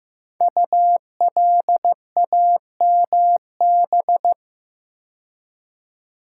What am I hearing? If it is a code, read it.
Morse code, heard as a keyed tone, about 15 wpm, 700 Hz. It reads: ULAMB
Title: Code Groups - 5 Letters - 15wpm